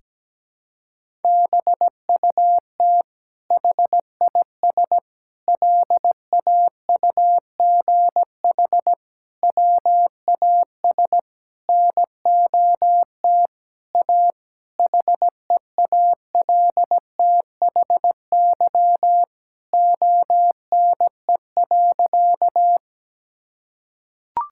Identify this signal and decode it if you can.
Morse code, heard as a keyed tone, about 17 wpm, 700 Hz. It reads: BUT HIS LAUGH WAS NOT A HEALTHY ONE.
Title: Sentences from Top 1000 Words - Review Part 5 - 17wpm